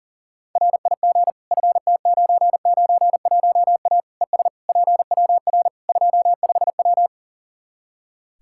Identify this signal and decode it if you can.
Morse code, heard as a keyed tone, about 40 wpm, 700 Hz. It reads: RIG FT991A ES PWR 25W
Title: Practice QSO - Exchange 2 - Info - 40wpm